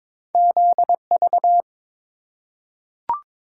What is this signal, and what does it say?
Morse code, heard as a keyed tone, about 22 wpm, 700 Hz. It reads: ZV
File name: practice / 22wpm / Two Letter-Number Combinations - 22wpm